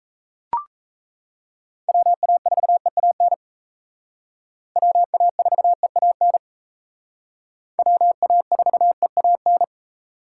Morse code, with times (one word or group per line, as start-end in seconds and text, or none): 1.9-3.3 WA4EAN
4.8-6.4 WA4EAN
7.8-9.6 WA4EAN